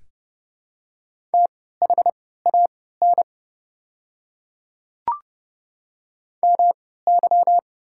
Morse code, heard as a keyed tone, about 30 wpm, spaced as for 17 wpm, 700 Hz. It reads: THAN MY
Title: 3x Character Spacing - Top 100 Words - 30wpm